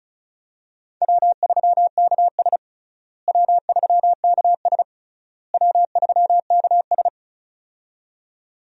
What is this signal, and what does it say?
Morse code, heard as a keyed tone, about 36 wpm, 700 Hz. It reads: W3KS W3KS W3KS
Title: USA Call Signs - 4 Characters - Repeated 3x - 35wpm